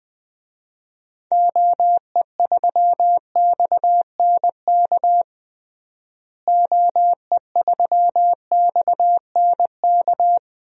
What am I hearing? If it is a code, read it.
Morse code, heard as a keyed tone, about 21 wpm, 700 Hz. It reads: OE3XNK OE3XNK
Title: DXCC-100 Call Signs - 6 Characters - Repeated 3x - 20wpm